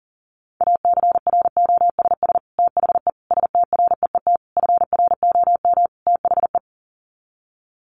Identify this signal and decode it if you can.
Morse code, heard as a keyed tone, about 40 wpm, 700 Hz. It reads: ACROSS THE STREET FROM THE
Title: Sets of 5 Words Encore 40wpm